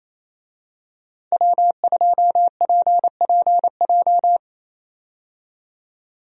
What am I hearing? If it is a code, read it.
Morse code, heard as a keyed tone, about 28 wpm, 700 Hz. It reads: W2PPJ
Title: USA Call Signs - 5 Characters - 28wpm